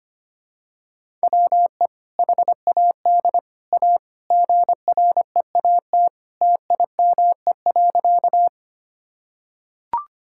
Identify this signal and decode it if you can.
Morse code, heard as a keyed tone, about 25 wpm, 700 Hz. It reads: WE HAD A GREAT TIME.
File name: practice / 25wpm / Sentences with 2-Word Phrases - 25wpm